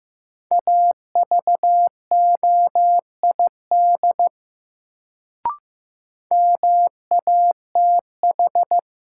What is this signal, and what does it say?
Morse code, heard as a keyed tone, about 15 wpm, 700 Hz. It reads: AVOID MATH
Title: Sentences from Top 1000 Words - Review Part 6 - New Word Drill - 15wpm